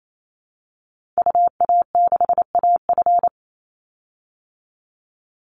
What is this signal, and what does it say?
Morse code, heard as a keyed tone, about 28 wpm, 700 Hz. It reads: UA6AF
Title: Call Signs Repeated 2x - 28wpm